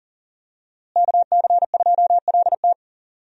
Morse code, heard as a keyed tone, about 40 wpm, 700 Hz. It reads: KC2LT
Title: USA Call Signs - 5 Characters - Repeated 2x - 40wpm